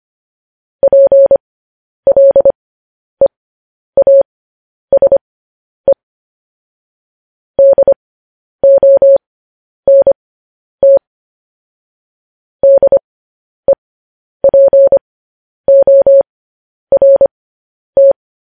Morse code, heard as a keyed tone, about 25 wpm, spaced as for 10 wpm, 550 Hz. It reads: PLEASE DONT DEPORT
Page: news headlines in morse code at 10 wpm